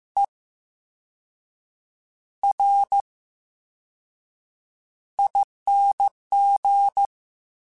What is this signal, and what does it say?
Morse code, heard as a keyed tone, about 15 wpm, spaced as for 6 wpm, 800 Hz. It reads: ERING
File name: Special INGs / Special INGs-mp3s / Slow ing @13